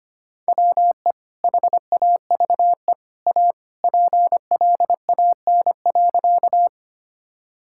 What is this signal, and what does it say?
Morse code, heard as a keyed tone, about 25 wpm, 700 Hz. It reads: WE HAVE A PLAN.
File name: practice / 25wpm / Sentences with 2-Word Phrases - 25wpm